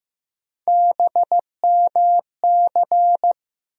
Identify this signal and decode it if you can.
Morse code, heard as a keyed tone, about 15 wpm, 700 Hz. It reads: BMC
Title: USA Call Sign Postfixes - 3 Letters 15wpm